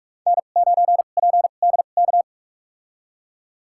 Morse code, heard as a keyed tone, about 42 wpm, spaced as for 31 wpm, 700 Hz. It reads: N9PDK
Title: USA Call Signs - 5 Characters - 2x Character Spacing - 45wpm